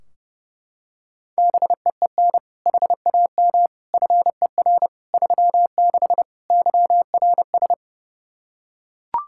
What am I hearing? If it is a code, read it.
Morse code, heard as a keyed tone, about 29 wpm, 700 Hz. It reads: BEEN HAM FER 36 YRS
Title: Practice QSO - Exchange 3 - More Info - 30wpm